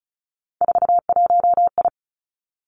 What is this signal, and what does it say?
Morse code, heard as a keyed tone, about 36 wpm, 700 Hz. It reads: 41I
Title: Three Challenging Letter-Number Combinations 35wpm